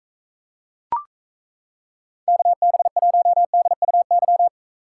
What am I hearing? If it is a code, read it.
Morse code, heard as a keyed tone, about 35 wpm, 700 Hz. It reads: KB1DUY